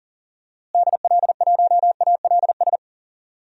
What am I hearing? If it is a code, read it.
Morse code, heard as a keyed tone, about 40 wpm, 700 Hz. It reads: DL1ALS
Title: DXCC-100 Call Signs - 6 Characters - Repeated 3x - 40wpm